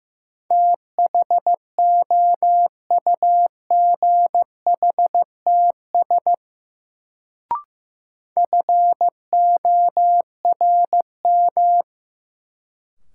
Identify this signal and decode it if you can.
Morse code, heard as a keyed tone, about 15 wpm, 700 Hz. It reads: THOUGHTS FORM
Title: Sentences from Top 1000 Words - Review Part 1 - New Word Drill - 15wpm